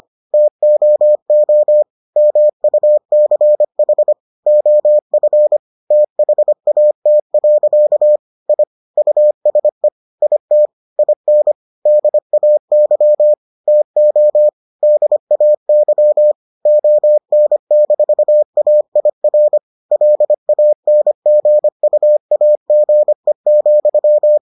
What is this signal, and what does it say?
Morse code, heard as a keyed tone, about 25 wpm, 600 Hz. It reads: TOO MUCH OF THAT. I USE IT IN DAY TO DAY ON-AIR LANGUAGE,